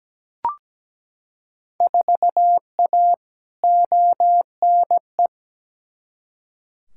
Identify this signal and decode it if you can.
Morse code, heard as a keyed tone, about 17 wpm, 700 Hz. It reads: 4A ONE
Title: Field Day Exchanges Encore - 17wpm